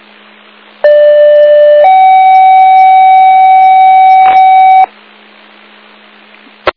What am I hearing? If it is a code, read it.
Two-tone page, heard as a one-second tone, then a three-second tone, about 590 then 760 Hz. It is heard from 0.8 to 4.8 s.